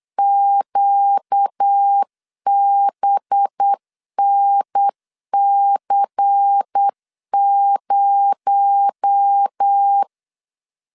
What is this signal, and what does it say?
Morse code, heard as a keyed tone, about 8 wpm, 800 Hz. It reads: QBNC0
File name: LEZIONI / 7 GF9Z / 7 GF9 8wpm 3545